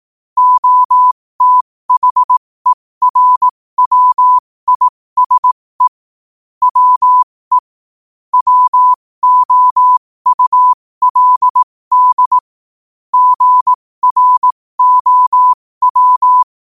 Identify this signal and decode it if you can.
Morse code, heard as a keyed tone, about 18 wpm, spaced as for 15 wpm, 1000 Hz. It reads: OTHERWISE WE WOULD GROW
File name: qotdfiles / 15wpm